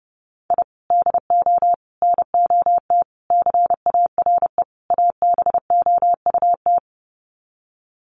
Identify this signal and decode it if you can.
Morse code, heard as a keyed tone, about 30 wpm, 700 Hz. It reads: I DO NOT CARE ABOUT
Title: Sets of 5 Words Encore2 30wpm